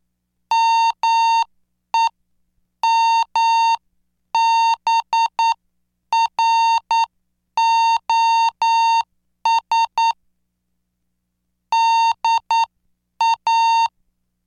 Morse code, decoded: MEMBROS DA